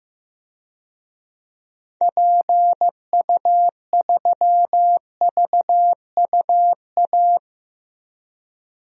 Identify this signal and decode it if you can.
Morse code, heard as a keyed tone, about 15 wpm, 700 Hz. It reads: PU3VUA
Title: DXCC-100 Call Signs - 6 Characters - 15wpm